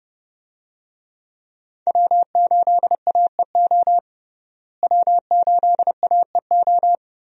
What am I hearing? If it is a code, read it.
Morse code, heard as a keyed tone, about 30 wpm, 700 Hz. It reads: W8AEO W8AEO
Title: USA Call Signs - 5 Characters - Repeated 2x - 30wpm